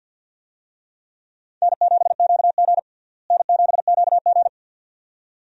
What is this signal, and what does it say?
Morse code, heard as a keyed tone, about 47 wpm, 700 Hz. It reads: N7QG N7QG
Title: USA Call Signs - 4 Characters - Repeated 2x - 50wpm